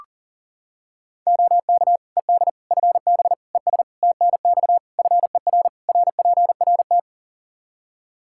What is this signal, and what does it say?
Morse code, heard as a keyed tone, about 40 wpm, 700 Hz. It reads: OK ED FB ES TNX FER RPRT
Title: K1USN SST QSO Practice 40wpm